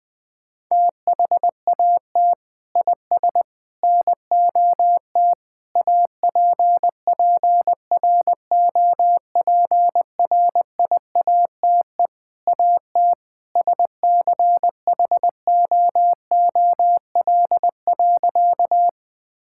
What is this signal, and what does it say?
Morse code, heard as a keyed tone, about 21 wpm, 700 Hz. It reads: THAT IS NOT APPROPRIATE AT SCHOOL.
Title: Sentences from Top 1000 Words - Review Part 5 - 20wpm